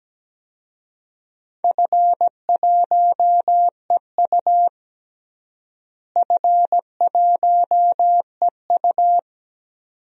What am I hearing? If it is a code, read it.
Morse code, heard as a keyed tone, about 17 wpm, 700 Hz. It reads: F1EU F1EU
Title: DXCC-100 Call Signs - 4 Characters - Repeated 2x - 17wpm